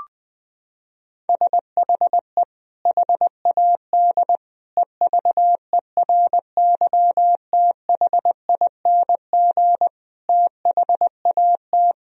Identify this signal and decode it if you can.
Morse code, heard as a keyed tone, about 21 wpm, 700 Hz. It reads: SHE HAD EVERYTHING THAT
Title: Sentences from Top 1000 Words - Review Part 4 - 20wpm